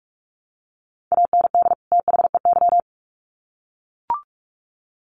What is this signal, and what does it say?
Morse code, heard as a keyed tone, about 44 wpm, 700 Hz. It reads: AND THEY
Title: Sets of 2 Words 45wpm